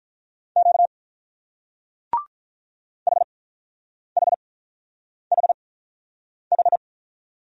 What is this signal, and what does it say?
Morse code, heard as a keyed tone, about 44 wpm, spaced as for 9 wpm, 700 Hz. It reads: Q HHHH